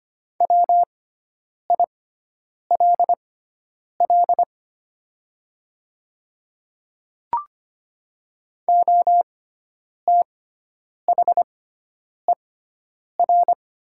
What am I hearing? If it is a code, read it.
Morse code, heard as a keyed tone, about 25 wpm, spaced as for 9 wpm, 700 Hz. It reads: WILL OTHER
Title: 6x Character Spacing - Top 100 Words - 25wpm